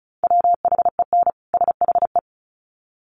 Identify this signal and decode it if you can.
Morse code, heard as a keyed tone, about 34 wpm, 700 Hz. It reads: WHEN SHE
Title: Sets of 2 Words 35wpm